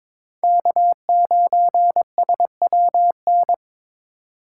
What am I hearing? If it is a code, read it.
Morse code, heard as a keyed tone, about 25 wpm, 700 Hz. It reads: K9SWN
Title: Speed Racing - USA 1x3 Call Signs 22wpm